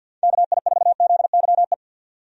Morse code, heard as a keyed tone, about 50 wpm, 700 Hz. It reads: KI4ZYE
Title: USA Call Signs - 6 Characters - 50wpm